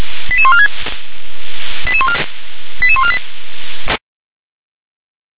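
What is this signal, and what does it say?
Five-tone selective call (ZVEI), heard as quick runs of five tones, tones of about 70 ms. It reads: 88146 88148
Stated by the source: ZVEI five-tone call received by SDR.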